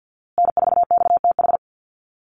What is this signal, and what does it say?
Morse code, heard as a keyed tone, about 50 wpm, 700 Hz. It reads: N4XTH
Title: USA Call Signs Repeated 3x 50wpm